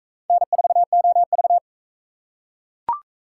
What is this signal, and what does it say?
Morse code, heard as a keyed tone, about 41 wpm, 700 Hz. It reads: N4OV